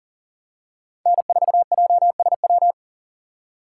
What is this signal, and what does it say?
Morse code, heard as a keyed tone, about 37 wpm, 700 Hz. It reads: N4JSW
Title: USA Call Signs - 5 Characters - 2x Character Spacing - 40wpm